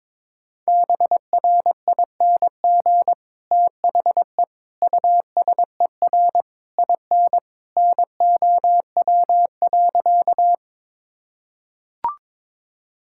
Morse code, heard as a keyed tone, about 22 wpm, 700 Hz. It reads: BRING THE USER IN NOW.
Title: Sentences from Top 1000 Words - Review Part 5 - 22wpm